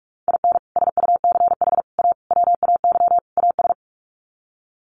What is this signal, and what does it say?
Morse code, heard as a keyed tone, about 44 wpm, 700 Hz. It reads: IN SUCH A WAY AS
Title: Sets of 5 Words Encore 45wpm